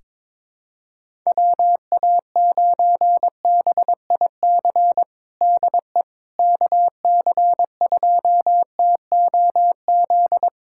Morse code, heard as a keyed tone, about 22 wpm, 700 Hz. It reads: WA9BIC DE KC2TOZ
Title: USA Callsign DE Callsign 22wpm